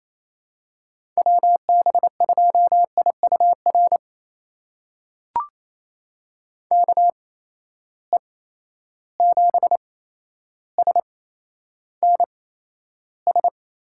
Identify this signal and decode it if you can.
Morse code, heard as a keyed tone, about 28 wpm, spaced as for 12 wpm, 700 Hz. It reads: WB2IUR KE7SNS
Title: USA Call Signs - 6 Characters - 8x Character Spacing - 28wpm